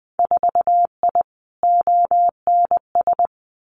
Morse code, heard as a keyed tone, about 20 wpm, 700 Hz. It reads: 4I ONS
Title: Winter Field Day Exchanges Encore 20wpm